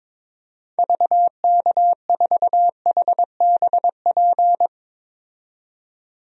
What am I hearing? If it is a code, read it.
Morse code, heard as a keyed tone, about 22 wpm, 700 Hz. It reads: VK4HBP
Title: DXCC-100 Call Signs - 6 Characters - 22wpm